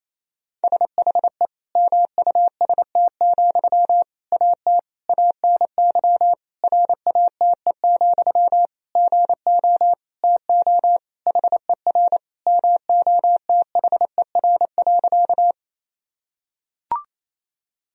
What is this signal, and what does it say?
Morse code, heard as a keyed tone, about 28 wpm, 700 Hz. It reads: SHE MUST, AT ANY RATE, GO TO HER MOTHER.
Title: Sentences from Top 1000 Words - Review Part 4 - 28wpm